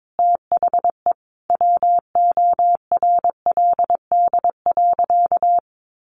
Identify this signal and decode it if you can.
Morse code, heard as a keyed tone, about 22 wpm, 700 Hz. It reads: THE WORLD.
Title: Sentences from Top 700 Words 22wpm